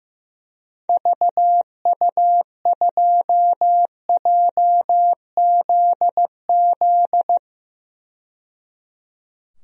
Morse code, heard as a keyed tone, about 15 wpm, 700 Hz. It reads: VU2JZZ
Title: DXCC-100 Call Signs - 6 Characters - 15wpm